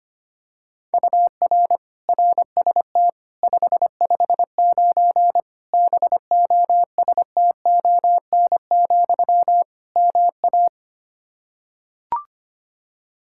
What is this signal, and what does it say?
Morse code, heard as a keyed tone, about 25 wpm, 700 Hz. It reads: UR RST 559 BOSTON, MA
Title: K1USN SST QSO Practice 25wpm